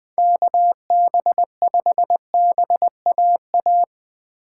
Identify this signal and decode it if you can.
Morse code, heard as a keyed tone, about 21 wpm, 700 Hz. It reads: KB5BAA
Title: Speed Racing - USA 2x3 Call Signs 17wpm